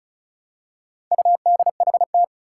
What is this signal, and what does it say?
Morse code, heard as a keyed tone, about 36 wpm, 700 Hz. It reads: UDHT
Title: Four Challenging Letter-Number Combinations 35wpm